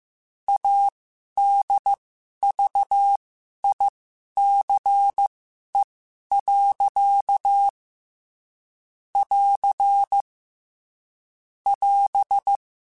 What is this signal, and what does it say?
Morse code, heard as a keyed tone, about 15 wpm, spaced as for 11 wpm, 800 Hz. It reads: ADVICE. <AR> <AS>
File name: Quotes / Quotes-mp3s / Advice @13